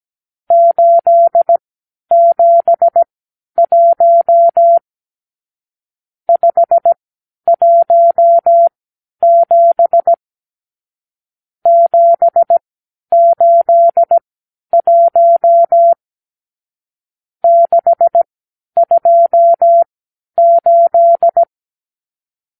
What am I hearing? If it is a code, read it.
Morse code, heard as a keyed tone, about 17 wpm, spaced as for 10 wpm, 680 Hz. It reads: 871 517 781 628
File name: arrl / Review (1-0)